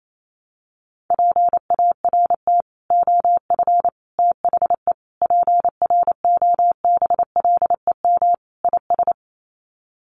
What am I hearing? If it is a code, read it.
Morse code, heard as a keyed tone, about 28 wpm, 700 Hz. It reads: PART OF THE PROBLEM IS
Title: Sets of 5 Words Encore2 28wpm